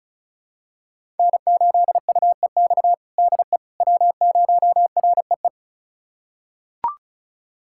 Morse code, heard as a keyed tone, about 34 wpm, 700 Hz. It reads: N8UEX DE W0REE
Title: USA Callsign DE Callsign - 5 Characters - 35wpm